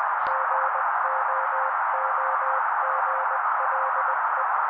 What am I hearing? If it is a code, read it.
Morse code, heard as a keyed tone, about 20 wpm, 530 Hz. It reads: GOOGLE
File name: daily / 18wpm noise